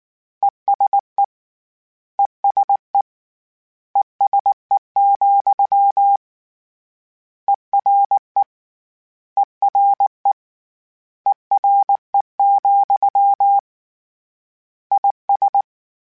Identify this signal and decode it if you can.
Morse code, heard as a keyed tone, about 19 wpm, 800 Hz. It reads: ESE ESE ESE, ERE ERE ERE, IS